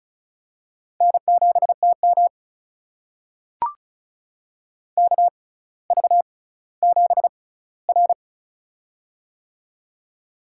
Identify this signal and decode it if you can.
Morse code, heard as a keyed tone, about 36 wpm, spaced as for 18 wpm, 700 Hz. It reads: N7TM KV7R
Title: USA Call Signs - 4 Characters - 6x Character Spacing - 35wpm